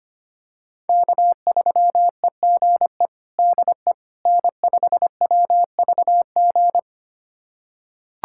Morse code, heard as a keyed tone, about 25 wpm, 700 Hz. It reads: K3EGE DE N5WVG
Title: USA Callsign DE Callsign - 5 Characters - 25wpm